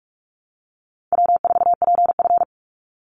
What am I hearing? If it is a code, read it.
Morse code, heard as a keyed tone, about 46 wpm, 700 Hz. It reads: W4PF
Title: Easy Call Signs - 45wpm